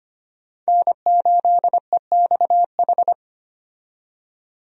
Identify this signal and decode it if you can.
Morse code, heard as a keyed tone, about 25 wpm, 700 Hz. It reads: N8EXH
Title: USA Call Signs - 5 Characters - 25wpm